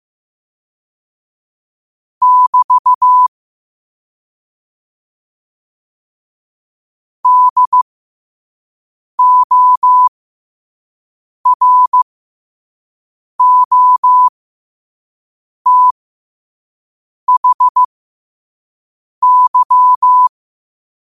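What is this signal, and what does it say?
Morse code, heard as a keyed tone, about 15 wpm, spaced as for 5 wpm, 1000 Hz. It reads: <BT> DOROTHY
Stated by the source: Morse code practice at 5 wpm.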